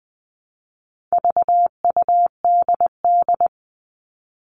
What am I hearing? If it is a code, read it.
Morse code, heard as a keyed tone, about 20 wpm, 700 Hz. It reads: VUDD
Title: All Cut Numbers - 4-Digit 20wpm